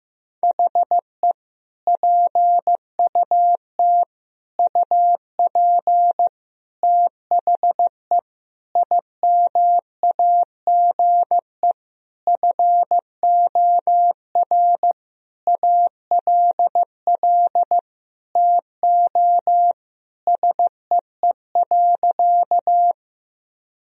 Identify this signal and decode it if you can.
Morse code, heard as a keyed tone, about 16 wpm, 700 Hz. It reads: HE PUT UP THE IMAGE FOR ALL TO SEE.
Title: Sentences from Top 1000 Words - Review Part 4 - 15wpm